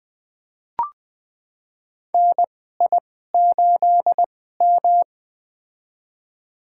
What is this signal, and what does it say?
Morse code, heard as a keyed tone, about 20 wpm, spaced as for 14 wpm, 700 Hz. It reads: NI8M
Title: USA Call Signs - 4 Characters - 2x Character Spacing - 20wpm